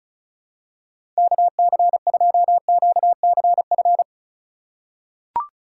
Morse code, heard as a keyed tone, about 36 wpm, 700 Hz. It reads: KC2QCF
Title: USA Call Signs - 6 Characters - 2x Character Spacing - 35wpm